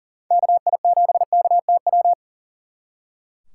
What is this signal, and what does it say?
Morse code, heard as a keyed tone, about 49 wpm, 700 Hz. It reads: KI7KTW
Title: Speed Racing - USA 2x3 Call Signs 40wpm